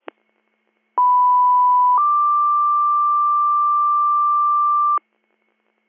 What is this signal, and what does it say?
Two-tone page, heard as a one-second tone, then a three-second tone, about 990 then 1180 Hz, with noise. It is heard from 1.0 to 5.0 s.